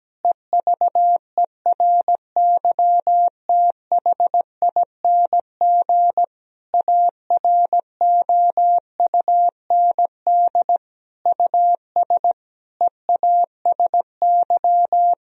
Morse code, heard as a keyed tone, about 17 wpm, 700 Hz. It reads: EVERYTHING AROUND US EASY
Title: Sentences from Top 1000 Words - Review Part 3 - 17wpm